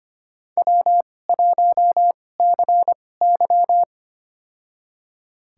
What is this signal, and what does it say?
Morse code, heard as a keyed tone, about 25 wpm, spaced as for 18 wpm, 700 Hz. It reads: W1CY
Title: USA Call Signs - 4 Characters - 2x Character Spacing - 25wpm